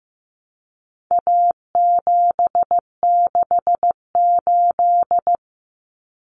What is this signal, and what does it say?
Morse code, heard as a keyed tone, about 15 wpm, 700 Hz. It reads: A768
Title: Common Cut Numbers - 4-Digit 15wpm